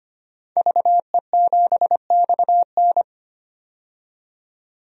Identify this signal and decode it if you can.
Morse code, heard as a keyed tone, about 25 wpm, 700 Hz. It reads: VE7XN
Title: DXCC-100 Call Signs - 5 Characters - 25wpm